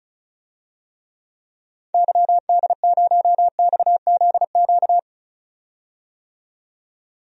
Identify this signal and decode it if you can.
Morse code, heard as a keyed tone, about 36 wpm, 700 Hz. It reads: YD0XZQ
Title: DXCC-100 Call Signs - 6 Characters - 35wpm